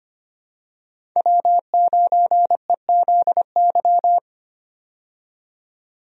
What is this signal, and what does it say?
Morse code, heard as a keyed tone, about 25 wpm, 700 Hz. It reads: W9EZY
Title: USA Call Signs - 5 Characters - 25wpm